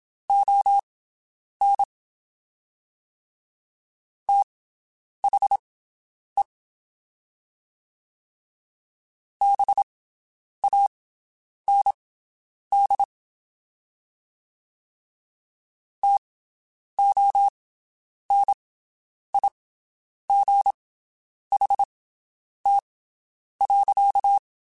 Morse code, heard as a keyed tone, about 27 wpm, spaced as for 9 wpm, 800 Hz. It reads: ON THE BAND TONIGHT.
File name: (spaced) / QSOs Spaced-mp3s / WD0BGP 1 @25 sp